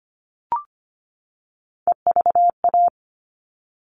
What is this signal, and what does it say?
Morse code, heard as a keyed tone, about 25 wpm, 700 Hz. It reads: EVA
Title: All Cut Numbers - 3-Digit 25wpm